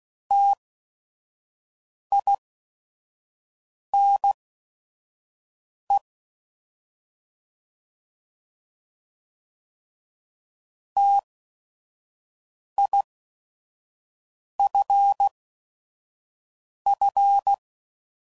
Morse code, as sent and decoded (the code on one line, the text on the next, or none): - .. -. / - .. ..-. ..-.
TINE TIFF